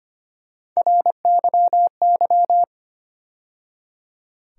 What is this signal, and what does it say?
Morse code, heard as a keyed tone, about 25 wpm, 700 Hz. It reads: RYY
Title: USA Call Sign Postfixes - 3 Letters 25wpm